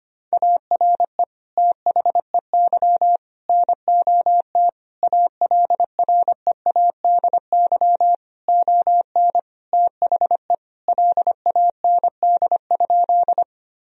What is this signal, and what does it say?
Morse code, heard as a keyed tone, about 25 wpm, 700 Hz. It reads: ARE THEY NOT ALREADY ON THE LAND?
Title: Sentences from Top 1000 Words - Review Part 1 - 25wpm